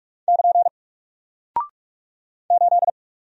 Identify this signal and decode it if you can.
Morse code, heard as a keyed tone, about 46 wpm, 700 Hz. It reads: <KN> 8
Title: Single Character (Letter:Number:Punctuation:Prosign) Alt 45wpm